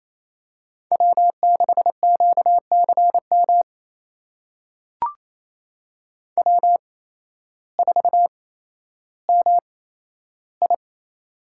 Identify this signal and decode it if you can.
Morse code, heard as a keyed tone, about 28 wpm, spaced as for 13 wpm, 700 Hz. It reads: W6QCM W4MI
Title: USA Call Signs - 8x Character Spacing - 28wpm